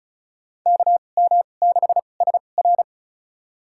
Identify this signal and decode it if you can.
Morse code, heard as a keyed tone, about 36 wpm, spaced as for 25 wpm, 700 Hz. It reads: KM6SR
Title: USA Call Signs - 5 Characters - 2x Character Spacing - 35wpm